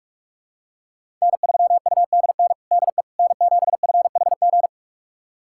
Morse code, heard as a keyed tone, about 46 wpm, 700 Hz. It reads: N3UDN DE N7FHG
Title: USA Callsign DE Callsign - 5 Characters - 45wpm